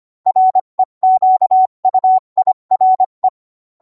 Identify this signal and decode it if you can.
Morse code, heard as a keyed tone, about 25 wpm, 750 Hz. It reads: REQUIRE